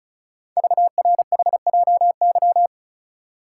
Morse code, heard as a keyed tone, about 36 wpm, 690 Hz. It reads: VRHJY
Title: Code Groups - 5 Letter:Numbers - 35wpm